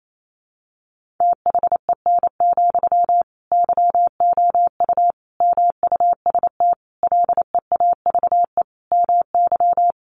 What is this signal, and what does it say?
Morse code, heard as a keyed tone, about 28 wpm, 700 Hz. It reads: THEN, YOU MUST LEAVE MY
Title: Sentences from Top 1500 Words 28wpm